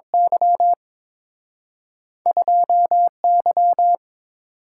Morse code, heard as a keyed tone, about 30 wpm, spaced as for 26 wpm, 700 Hz. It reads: Y 2Y
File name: Speed-Racing - Two Letter-Number Combinations - 22wpm